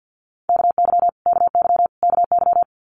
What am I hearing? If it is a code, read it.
Morse code, heard as a keyed tone, about 50 wpm, 700 Hz. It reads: KY KY KY
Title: Multi-letter Phonemes Repeated 3x 50wpm